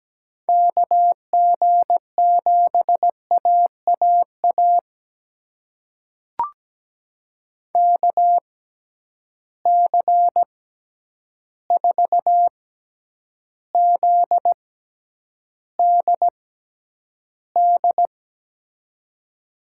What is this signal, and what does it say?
Morse code, heard as a keyed tone, about 17 wpm, spaced as for 9 wpm, 700 Hz. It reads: KG7AAA KC4ZDD
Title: USA Call Signs - 6 Characters - 6x Character Spacing - 17wpm